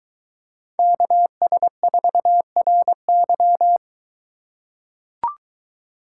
Morse code, heard as a keyed tone, about 22 wpm, 700 Hz. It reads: KS4RY